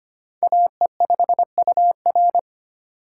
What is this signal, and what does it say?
Morse code, heard as a keyed tone, about 25 wpm, 700 Hz. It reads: AE5UR
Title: USA Call Signs - 5 Characters - 4x Character Spacing - 25wpm